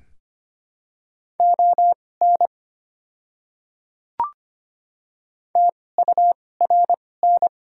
Morse code, heard as a keyed tone, about 25 wpm, spaced as for 18 wpm, 700 Hz. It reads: ON TURN
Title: 2x Character Spacing - Top 300 Words - 25wpm